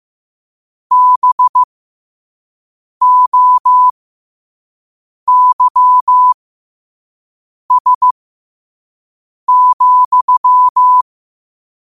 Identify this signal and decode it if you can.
Morse code, heard as a keyed tone, about 15 wpm, spaced as for 5 wpm, 1000 Hz. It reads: BOYS,